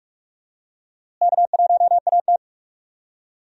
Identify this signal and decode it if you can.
Morse code, heard as a keyed tone, about 46 wpm, 700 Hz. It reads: K1AT
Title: USA Call Signs - 4 Characters - Repeated 3x - 45wpm